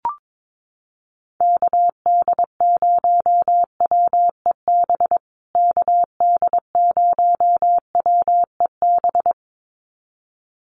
Morse code, heard as a keyed tone, about 22 wpm, 700 Hz. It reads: KD0WEB KD0WEB
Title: USA Call Signs Repeated 2x 22wpm